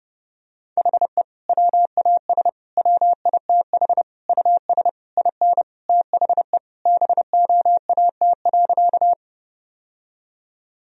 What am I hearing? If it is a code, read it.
Morse code, heard as a keyed tone, about 30 wpm, 700 Hz. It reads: HE WAS WITH US IN THE BOAT.
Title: Sentences from Top 1000 Words - Review Part 3 - 30wpm